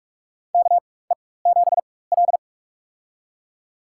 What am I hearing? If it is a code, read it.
Morse code, heard as a keyed tone, about 46 wpm, spaced as for 21 wpm, 700 Hz. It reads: KE7L
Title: USA Call Signs - 4 Characters - 4x Character Spacing - 45wpm